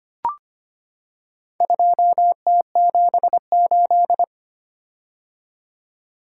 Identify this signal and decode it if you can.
Morse code, heard as a keyed tone, about 25 wpm, 700 Hz. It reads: 2T78
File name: practice / 25wpm / Four Letter-Number Combinations - 25wpm